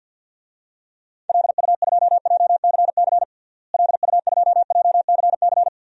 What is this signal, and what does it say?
Morse code, heard as a keyed tone, about 48 wpm, 700 Hz. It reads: LU2JCC LU2JCC
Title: DXCC-100 Call Signs - 6 Characters - Repeated 2x - 50wpm